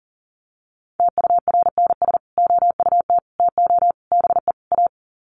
Morse code, heard as a keyed tone, about 38 wpm, 700 Hz. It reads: TURNS OUT TO BE A
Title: Sets of 5 Words Encore 40wpm